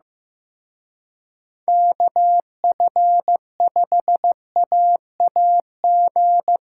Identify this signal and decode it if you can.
Morse code, heard as a keyed tone, about 18 wpm, 700 Hz. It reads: KF5AAG
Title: Speed Racing - USA 2x3 Call Signs 15wpm